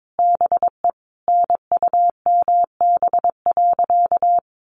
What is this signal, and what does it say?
Morse code, heard as a keyed tone, about 22 wpm, 700 Hz. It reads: BE NUMB.
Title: A Sound Of Thunder 22wpm